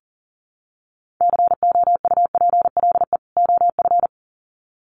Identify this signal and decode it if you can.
Morse code, heard as a keyed tone, about 40 wpm, 700 Hz. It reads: COUPLE OF
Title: Sets of 2 Words 40wpm